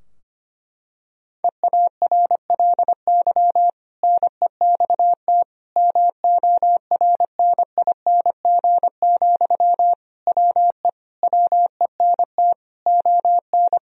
Morse code, heard as a keyed tone, about 25 wpm, 700 Hz. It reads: EARLY NEXT MORNING, WE WENT ON
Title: Sentences from Top 1000 Words - Review Part 4 - 25wpm